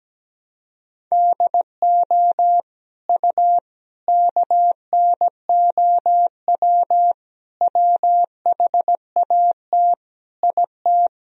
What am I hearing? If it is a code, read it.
Morse code, heard as a keyed tone, about 17 wpm, 700 Hz. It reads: DO U KNOW WHAT IT
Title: Sentences from Top 100 Words and Abbreviations 17wpm